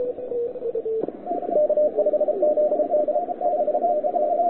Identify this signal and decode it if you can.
Morse code, heard as a keyed tone, about 34 wpm, 600 Hz, with radio noise. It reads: K5ZD 5NN
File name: wwcw08 / k5zd wwcw08 1300z